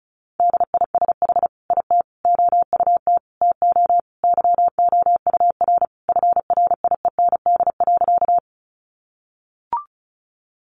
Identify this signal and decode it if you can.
Morse code, heard as a keyed tone, about 34 wpm, 700 Hz. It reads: DISH IT OUT TO YOUR FRIEND.
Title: Sentences from Top 1300 Words 35wpm